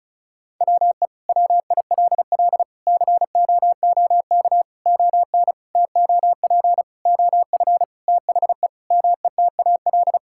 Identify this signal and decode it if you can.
Morse code, heard as a keyed tone, about 34 wpm, 700 Hz. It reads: WE WILL COOK ON TOP OF THE METAL
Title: Sentences from Top 1000 Words - Review Part 6 - 35wpm